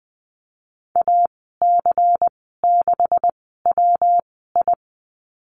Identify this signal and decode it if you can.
Morse code, heard as a keyed tone, about 19 wpm, spaced as for 14 wpm, 700 Hz. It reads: AC6WI
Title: 179 Lesson 29g - Slash - Call Signs 20wpm